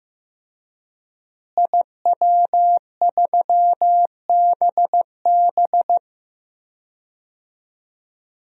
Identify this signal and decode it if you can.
Morse code, heard as a keyed tone, about 15 wpm, 700 Hz. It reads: IW3BB
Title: DXCC-100 Call Signs - 5 Characters - Repeated 3x - 15wpm